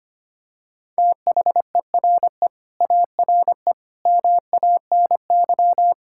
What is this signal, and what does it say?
Morse code, heard as a keyed tone, about 25 wpm, 700 Hz. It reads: THERE ARE MANY
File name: practice / 25wpm / Sentences with 2-Word Phrases - 25wpm